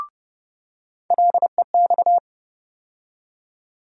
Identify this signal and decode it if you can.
Morse code, heard as a keyed tone, about 30 wpm, 700 Hz. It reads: LEX
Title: USA Call Sign Postfixes - 3 Letters 30wpm